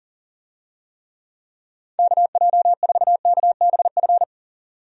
Morse code, heard as a keyed tone, about 38 wpm, 700 Hz. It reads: KJ4KBF
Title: USA Call Signs - 6 Characters - 40wpm